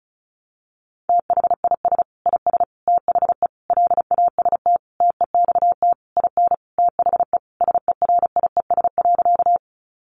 Morse code, heard as a keyed tone, about 34 wpm, 700 Hz. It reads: THIS IS THE LAST TEXT IN THE SERIES.
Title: Sentences from Top 1200 Words 35wpm